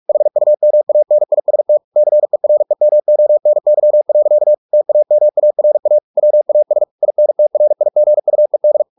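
Morse code, heard as a keyed tone, about 44 wpm, 600 Hz. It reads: HUMANIST CEREMONY. TAMARA WAS INTRIGUED
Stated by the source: Morse practice at 45 wpm, the novel Never by Ken Follett in English, from CW med Gnister.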